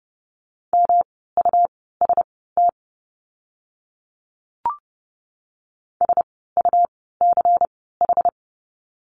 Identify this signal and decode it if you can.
Morse code, heard as a keyed tone, about 30 wpm, spaced as for 17 wpm, 700 Hz. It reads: MUST SUCH
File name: course / 30:10wpm / 237 Lesson 38f - Letter q - Words 30wpm